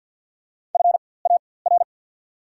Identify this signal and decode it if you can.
Morse code, heard as a keyed tone, about 54 wpm, spaced as for 24 wpm, 700 Hz. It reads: FAR
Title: 4x Character Spacing - Top 300 Words - 50wpm